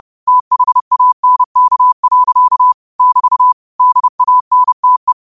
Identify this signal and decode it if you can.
Morse code, heard as a keyed tone, about 30 wpm, 1000 Hz. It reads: THANK. <BT> DANTE